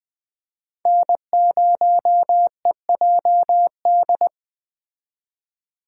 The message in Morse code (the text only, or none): N0EJD